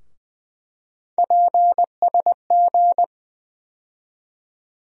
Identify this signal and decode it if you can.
Morse code, heard as a keyed tone, about 20 wpm, 700 Hz. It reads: PSG
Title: USA Call Sign Postfixes - 3 Letters 20wpm